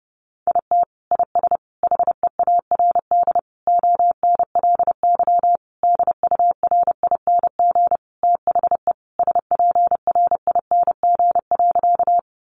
Morse code, heard as a keyed tone, about 30 wpm, 700 Hz. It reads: IT IS HEARD ONLY DURING THE SPRING.
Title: Sentences from Top 1200 Words 30wpm